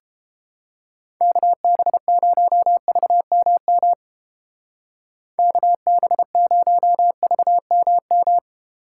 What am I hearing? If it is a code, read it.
Morse code, heard as a keyed tone, about 28 wpm, 700 Hz. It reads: KB0VMM KB0VMM